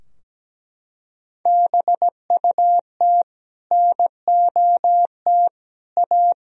Morse code, heard as a keyed tone, about 17 wpm, 700 Hz. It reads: BUT NOT A
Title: Sentences from Top 1000 Words - Review Part 4 - 17wpm